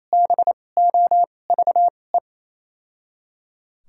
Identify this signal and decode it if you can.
Morse code, heard as a keyed tone, about 28 wpm, spaced as for 20 wpm, 700 Hz. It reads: BOVE
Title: 2x Character Spacing - Top 500 Words - 28wpm